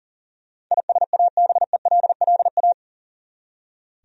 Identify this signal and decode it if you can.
Morse code, heard as a keyed tone, about 40 wpm, 700 Hz. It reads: ISABELLA
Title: USA Common Names - 40wpm